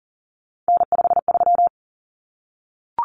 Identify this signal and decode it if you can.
Morse code, heard as a keyed tone, about 40 wpm, 700 Hz. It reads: N53
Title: Common Cut Numbers - 3-Digit 40wpm